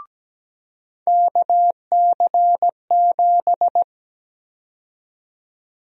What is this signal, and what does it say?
Morse code, heard as a keyed tone, about 17 wpm, 700 Hz. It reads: KC7
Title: USA Call Sign Prefixes - 17wpm